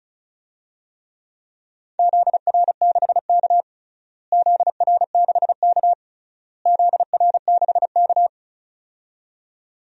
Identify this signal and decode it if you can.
Morse code, heard as a keyed tone, about 36 wpm, 700 Hz. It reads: ZR6K ZR6K ZR6K
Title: DXCC-100 Call Signs - 4 Characters - Repeated 3x - 35wpm